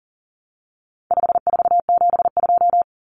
Morse code, heard as a keyed tone, about 40 wpm, 700 Hz. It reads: 5472
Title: Common Cut Numbers - 4-Digit 40wpm